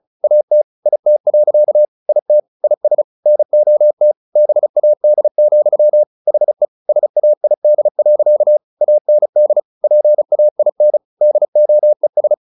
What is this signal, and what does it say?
Morse code, heard as a keyed tone, about 34 wpm, 600 Hz. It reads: AT IT. IT IS NOT BAD, HE SAID. AND PAIN DOES
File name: the old man and the sea 2 35wpm